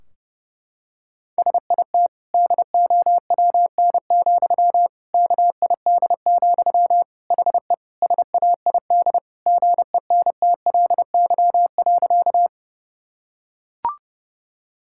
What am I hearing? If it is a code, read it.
Morse code, heard as a keyed tone, about 32 wpm, 700 Hz. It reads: SIT DOWN, KID, HE SAID GENTLY.
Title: Sentences from Top 1600 Words 30wpm